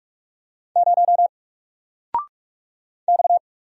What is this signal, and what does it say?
Morse code, heard as a keyed tone, about 44 wpm, 700 Hz. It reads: 0 X
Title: Single Letter-Number 45wpm